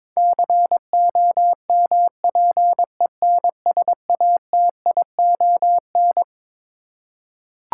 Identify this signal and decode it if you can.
Morse code, heard as a keyed tone, about 22 wpm, 700 Hz. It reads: COMPENSATION
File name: Twelve Letter Words - 22wpm